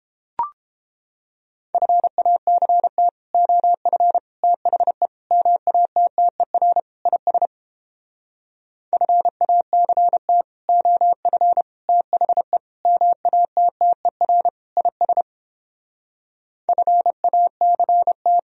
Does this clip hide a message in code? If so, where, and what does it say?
Yes: Morse code from 1.7 to 18.4 s, heard as a keyed tone, about 31 wpm, 700 Hz. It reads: FACT OF THE MATTER IS FACT OF THE MATTER IS FACT